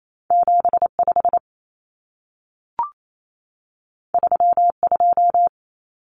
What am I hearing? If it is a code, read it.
Morse code, heard as a keyed tone, about 28 wpm, 700 Hz. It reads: 75 32
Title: Common Cut Numbers - 2-Digit 28wpm